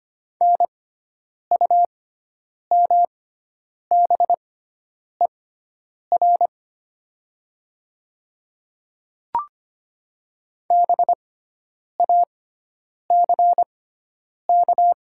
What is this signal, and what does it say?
Morse code, heard as a keyed tone, about 25 wpm, spaced as for 9 wpm, 690 Hz. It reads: NUMBER BACK
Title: 6x Character Spacing - Top 200 Words - 25wpm